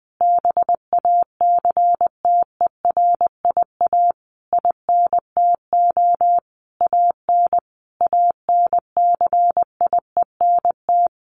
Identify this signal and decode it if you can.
Morse code, heard as a keyed tone, about 20 wpm, 700 Hz. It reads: BACTERIA INTO AN ANCIENT
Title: A Sound Of Thunder 20wpm